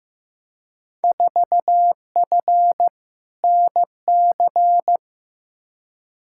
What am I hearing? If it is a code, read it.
Morse code, heard as a keyed tone, about 15 wpm, 700 Hz. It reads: 4F NC
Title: Field Day Exchanges - 15wpm